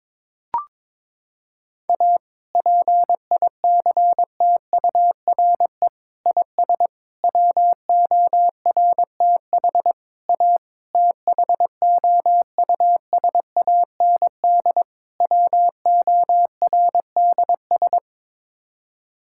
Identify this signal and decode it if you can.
Morse code, heard as a keyed tone, about 22 wpm, 700 Hz. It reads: A PICTURE IS WORTH A THOUSAND WORDS
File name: practice / 22wpm / Sets of 7 Words - 22wpm